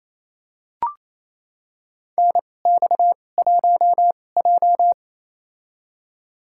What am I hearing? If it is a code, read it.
Morse code, heard as a keyed tone, about 27 wpm, spaced as for 20 wpm, 690 Hz. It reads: NX1J
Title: USA Call Signs - 4 Characters - 2x Character Spacing - 28wpm